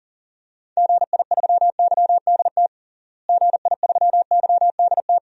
Morse code, heard as a keyed tone, about 37 wpm, 700 Hz. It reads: GI3YDT GI3YDT
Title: DXCC-100 Call Signs - 6 Characters - Repeated 2x - 40wpm